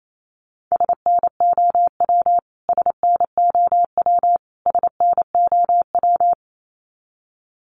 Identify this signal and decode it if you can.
Morse code, heard as a keyed tone, about 28 wpm, 700 Hz. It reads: SNOW SNOW SNOW
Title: Top 500 Words Repeated 28wpm